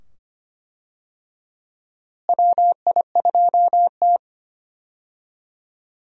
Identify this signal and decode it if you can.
Morse code, heard as a keyed tone, about 25 wpm, 700 Hz. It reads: WI2T
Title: USA Call Signs - 4 Characters - Repeated 2x - 25wpm